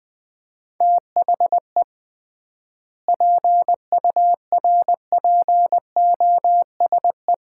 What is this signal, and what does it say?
Morse code, heard as a keyed tone, about 20 wpm, 700 Hz. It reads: THE PURPOSE
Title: 3x Word-Spacing - Sets of 5 Words - 20wpm